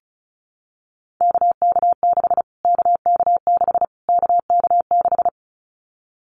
Morse code, heard as a keyed tone, about 34 wpm, 700 Hz. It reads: KK6 KK6 KK6
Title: Callsign Prefixes for Top 25 DXCC - Repeated 3x - 35wpm